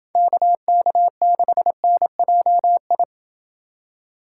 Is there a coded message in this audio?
Yes: Morse code from 0.2 to 3.0 s, heard as a keyed tone, about 27 wpm, 700 Hz. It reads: KK6NJI